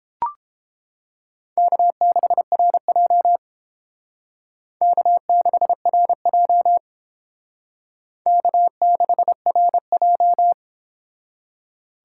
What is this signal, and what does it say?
Morse code, heard as a keyed tone, about 31 wpm, 700 Hz. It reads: K6RJ K6RJ K6RJ